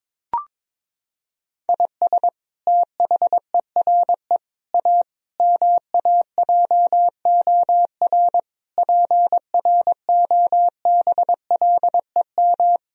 Morse code, heard as a keyed tone, about 22 wpm, 700 Hz. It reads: IS THERE A MAJOR PROBLEM
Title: Sentences from Top 1000 Words - Review Part 4 - 22wpm